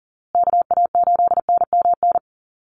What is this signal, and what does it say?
Morse code, heard as a keyed tone, about 40 wpm, 700 Hz. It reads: KA8NMN
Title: USA Call Signs Repeated 3x 40wpm